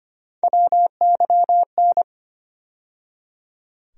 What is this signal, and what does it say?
Morse code, heard as a keyed tone, about 25 wpm, 700 Hz. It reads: WYN
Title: USA Call Sign Postfixes - 3 Letters 25wpm